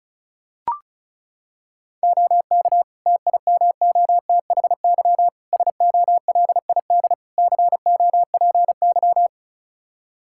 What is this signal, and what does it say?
Morse code, heard as a keyed tone, about 34 wpm, 700 Hz. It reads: OK TIMOTHY SOLID COPY
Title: K1USN SST QSO Practice 35wpm